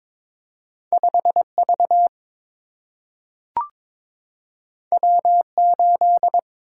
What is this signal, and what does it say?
Morse code, heard as a keyed tone, about 22 wpm, 700 Hz. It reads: 5 V W 8
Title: Single-Letter Single-Number 22wpm